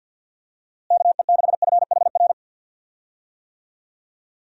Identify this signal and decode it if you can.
Morse code, heard as a keyed tone, about 49 wpm, 700 Hz. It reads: KE6FHR
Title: USA Call Signs - 6 Characters - Repeated 2x - 50wpm